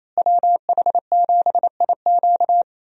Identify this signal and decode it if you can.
Morse code, heard as a keyed tone, about 28 wpm, 700 Hz. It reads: WH7IQ
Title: USA Call Signs - 5 Characters - Repeated 3x - 28wpm